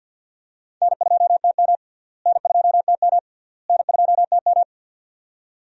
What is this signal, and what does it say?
Morse code, heard as a keyed tone, about 51 wpm, 700 Hz. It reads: N2TM N2TM N2TM
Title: USA Call Signs - 4 Characters - Repeated 3x - 50wpm